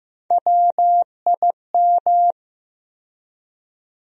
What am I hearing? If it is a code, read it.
Morse code, heard as a keyed tone, about 15 wpm, 700 Hz. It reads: WIM
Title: USA Call Sign Postfixes - 3 Letters 15wpm